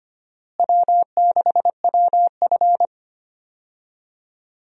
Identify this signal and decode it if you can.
Morse code, heard as a keyed tone, about 25 wpm, 700 Hz. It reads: W6WF
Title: USA Call Signs - 4 Characters - Repeated 2x - 25wpm